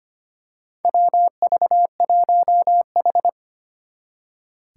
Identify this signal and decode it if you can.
Morse code, heard as a keyed tone, about 25 wpm, 700 Hz. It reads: WV1H
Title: Four Challenging Letter-Number Combinations 25wpm